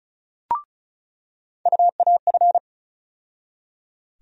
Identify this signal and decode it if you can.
Morse code, heard as a keyed tone, about 36 wpm, 700 Hz. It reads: UAF
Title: USA Call Sign Postfixes - 3 Letters 35wpm